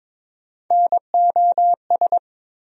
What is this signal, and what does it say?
Morse code, heard as a keyed tone, about 22 wpm, 700 Hz. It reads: NOS
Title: USA Call Sign Postfixes - 3 Letters 22wpm